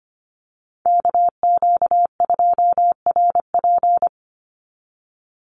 Morse code, heard as a keyed tone, about 25 wpm, 700 Hz. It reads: KQ2RP
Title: NA SOTA Chaser Callsigns 25wpm